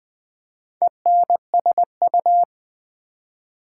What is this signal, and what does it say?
Morse code, heard as a keyed tone, about 20 wpm, 700 Hz. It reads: ENSU
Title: Build A Word - 2 Letters At A Time - 6-Letter Words - 20wpm